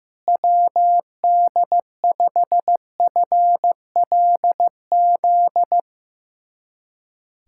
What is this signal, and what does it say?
Morse code, heard as a keyed tone, about 15 wpm, 700 Hz. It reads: WD5FLZ
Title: USA Callsign DE Callsign - 6 Characters - 15wpm